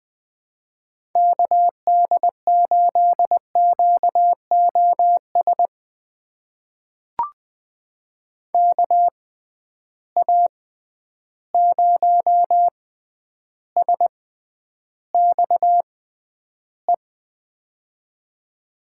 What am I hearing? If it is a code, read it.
Morse code, heard as a keyed tone, about 21 wpm, spaced as for 10 wpm, 700 Hz. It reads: KD8QOS KA0SXE
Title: USA Call Signs - 6 Characters - 6x Character Spacing - 20wpm